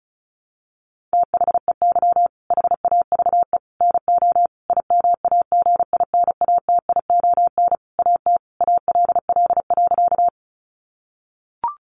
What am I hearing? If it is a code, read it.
Morse code, heard as a keyed tone, about 34 wpm, 700 Hz. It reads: THEY HAVE NO IMAGINATION AT ALL.
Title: Sentences from Top 1300 Words 35wpm